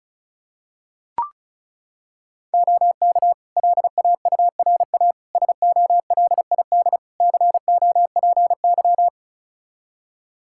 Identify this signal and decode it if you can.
Morse code, heard as a keyed tone, about 36 wpm, 700 Hz. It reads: OK LAURA SOLID COPY
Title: K1USN SST QSO Practice 35wpm